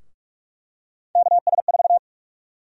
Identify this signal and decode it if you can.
Morse code, heard as a keyed tone, about 42 wpm, 700 Hz. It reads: KS4
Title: USA Call Sign Prefixes - 45wpm